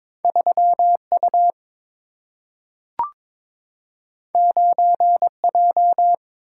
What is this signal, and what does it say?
Morse code, heard as a keyed tone, about 22 wpm, 700 Hz. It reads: 3 U 9 J
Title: Single-Letter Single-Number 22wpm